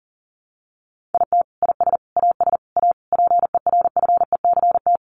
Morse code, heard as a keyed tone, about 40 wpm, 700 Hz. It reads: IT IS AS A PERFECT
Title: Sentences from Top 1000 Words 40wpm